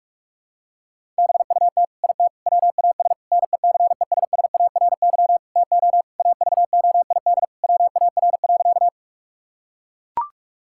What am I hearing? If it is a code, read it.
Morse code, heard as a keyed tone, about 41 wpm, 700 Hz. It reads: BUT IT WAS NECESSARY TO AVOID WAR.
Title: Sentences from Top 1000 Words - Review Part 6 - 45wpm